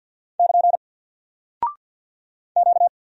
Morse code, heard as a keyed tone, about 50 wpm, 700 Hz. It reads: <KN> Q
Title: Single Character (Letter:Number:Punctuation:Prosign) 50wpm